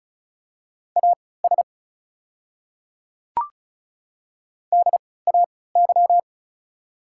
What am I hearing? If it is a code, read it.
Morse code, heard as a keyed tone, about 36 wpm, spaced as for 20 wpm, 690 Hz. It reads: AS DAY
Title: 3x Character Spacing - Top 200 Words - 35wpm